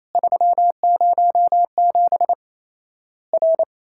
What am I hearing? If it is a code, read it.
Morse code, heard as a keyed tone, about 28 wpm, 700 Hz. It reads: 307 R
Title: Worked All Europe DX Contest - QTC Exchange - 28wpm